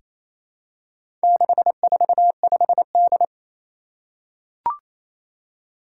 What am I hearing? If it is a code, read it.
Morse code, heard as a keyed tone, about 27 wpm, 700 Hz. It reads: 645D
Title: Four Challenging Letter-Number Combinations 28wpm